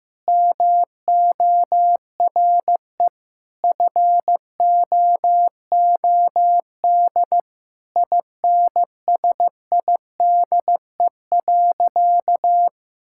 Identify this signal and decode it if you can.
Morse code, heard as a keyed tone, about 15 wpm, 700 Hz. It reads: MORE FOOD INSIDE.
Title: Sentences from Top 1000 Words - Review Part 6 - 15wpm